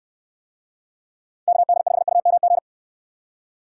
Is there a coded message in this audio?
Yes: Morse code from 1.5 to 2.6 s, heard as a keyed tone, about 62 wpm, 700 Hz. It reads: KD5UMG